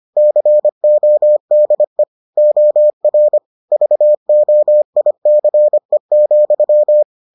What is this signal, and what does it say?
Morse code, heard as a keyed tone, about 25 wpm, 600 Hz. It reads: CODE OR VOICE,